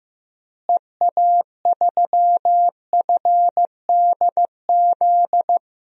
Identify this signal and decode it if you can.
Morse code, heard as a keyed tone, about 15 wpm, 700 Hz. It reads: EA3FDZ
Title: DXCC-100 Call Signs - 6 Characters - Repeated 3x - 15wpm